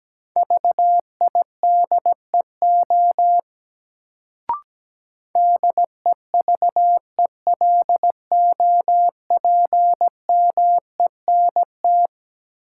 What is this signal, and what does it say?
Morse code, heard as a keyed tone, about 17 wpm, 700 Hz. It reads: VIDEO DEVELOPMENT
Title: Sentences from Top 1000 Words - Review Part 4 - New Word Drill - 17wpm